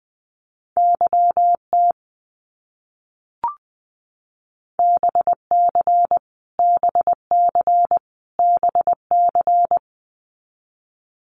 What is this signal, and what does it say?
Morse code, heard as a keyed tone, about 20 wpm, 700 Hz. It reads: YT BC BC BC
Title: Canadian Provinces and Territories - Repeated 3x - 20wpm